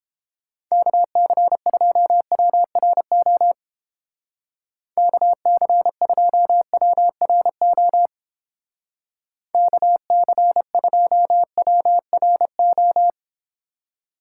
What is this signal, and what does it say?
Morse code, heard as a keyed tone, about 29 wpm, 700 Hz. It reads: KC2WRO KC2WRO KC2WRO